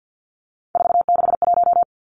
Morse code, heard as a keyed tone, about 50 wpm, 700 Hz. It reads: 461
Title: Numbers - 3-Digit 50wpm